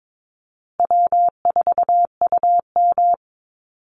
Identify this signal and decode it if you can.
Morse code, heard as a keyed tone, about 22 wpm, 700 Hz. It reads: W4UM
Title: Easy Call Signs Repeated 3x - 22wpm